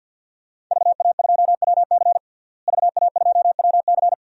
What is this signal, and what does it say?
Morse code, heard as a keyed tone, about 52 wpm, 700 Hz. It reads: VA2WC VA2WC
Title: DXCC-100 Call Signs - 5 Characters - Repeated 3x - 50wpm